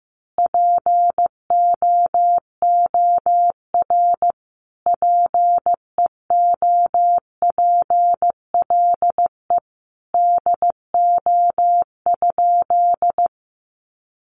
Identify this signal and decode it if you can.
Morse code, heard as a keyed tone, about 15 wpm, 700 Hz. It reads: POOR PEOPLE DO?
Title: Sentences from Top 1100 Words 15wpm